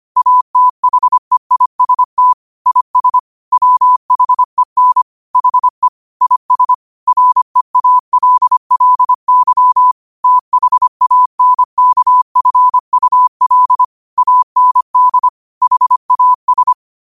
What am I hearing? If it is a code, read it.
Morse code, heard as a keyed tone, about 25 wpm, 1000 Hz. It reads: ATHEIST IS WHEN HE IS REALLY THANKFUL AND HAS